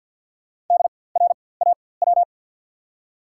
Morse code, heard as a keyed tone, about 52 wpm, spaced as for 24 wpm, 700 Hz. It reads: DRAW
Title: 4x Character Spacing - Top 300 Words - 50wpm